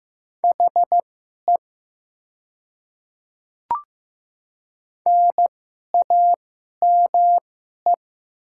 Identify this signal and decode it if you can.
Morse code, heard as a keyed tone, about 15 wpm, spaced as for 11 wpm, 700 Hz. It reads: HE NAME
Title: 2x Character Spacing - Top 200 Words - 15wpm